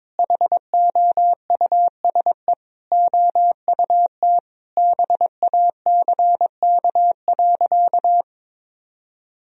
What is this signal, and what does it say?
Morse code, heard as a keyed tone, about 22 wpm, 700 Hz. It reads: HOUSE OUT BACK.
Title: Sentences from Top 1000 Words - Review Part 2 - 22wpm